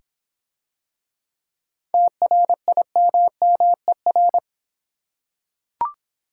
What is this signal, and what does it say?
Morse code, heard as a keyed tone, about 26 wpm, 700 Hz. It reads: TRIMMER